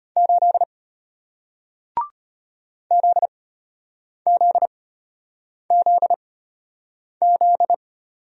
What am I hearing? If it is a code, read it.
Morse code, heard as a keyed tone, about 33 wpm, spaced as for 8 wpm, 700 Hz. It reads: 8 ZZZZ